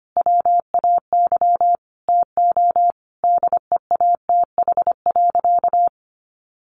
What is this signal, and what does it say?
Morse code, heard as a keyed tone, about 25 wpm, 700 Hz. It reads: WAY TO DEATH.
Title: Sentences from Top 1400 Words 25wpm